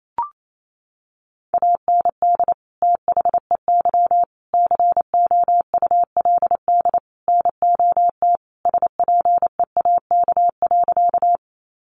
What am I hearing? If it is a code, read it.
Morse code, heard as a keyed tone, about 28 wpm, 700 Hz. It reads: AND THEY COULD NOT SPEAK.
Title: Sentences from Top 1100 Words 28wpm